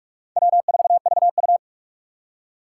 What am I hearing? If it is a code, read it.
Morse code, heard as a keyed tone, about 41 wpm, 700 Hz. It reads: W4VU